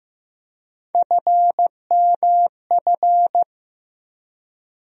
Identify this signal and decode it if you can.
Morse code, heard as a keyed tone, about 15 wpm, 700 Hz. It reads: FMF
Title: USA Call Sign Postfixes - 3 Letters 15wpm